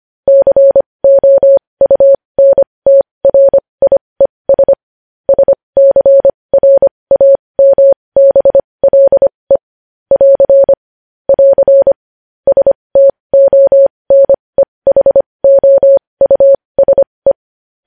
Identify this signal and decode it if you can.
Morse code, heard as a keyed tone, about 25 wpm, spaced as for 20 wpm, 550 Hz. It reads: COUNTRIES SCRAMBLE <AR> <AR> STONEHOUSE
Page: news headlines in morse code at 20 wpm